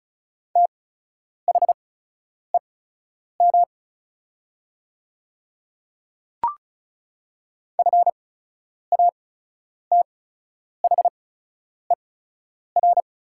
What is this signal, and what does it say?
Morse code, heard as a keyed tone, about 36 wpm, spaced as for 10 wpm, 700 Hz. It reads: THEM FATHER
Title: 8x Character Spacing - Top 200 Words - 35wpm